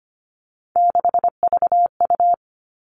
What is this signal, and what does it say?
Morse code, heard as a keyed tone, about 25 wpm, 700 Hz. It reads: 6VU
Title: All Cut Numbers - 3-Digit 25wpm